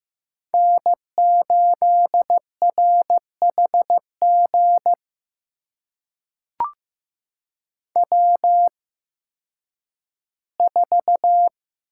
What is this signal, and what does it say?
Morse code, heard as a keyed tone, about 15 wpm, spaced as for 10 wpm, 700 Hz. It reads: N8RHG W4
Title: USA Call Signs - 5 Characters - 8x Character Spacing - 15wpm